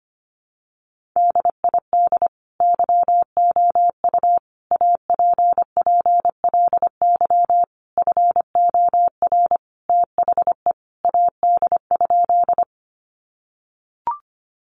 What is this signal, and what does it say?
Morse code, heard as a keyed tone, about 25 wpm, 700 Hz. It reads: DID YOU APPLY FOR THE AD?
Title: Sentences from Top 1300 Words 25wpm